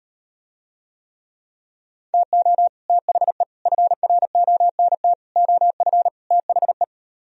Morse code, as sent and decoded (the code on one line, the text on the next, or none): - --- / - .... / ..-. .-. --- -. - / --- ..-. / - .... .
TO THE FRONT OF THE